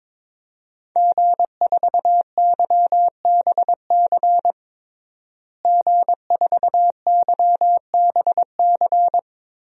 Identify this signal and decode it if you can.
Morse code, heard as a keyed tone, about 22 wpm, 700 Hz. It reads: G4YBC G4YBC
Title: DXCC-100 Call Signs - 5 Characters - Repeated 2x - 22wpm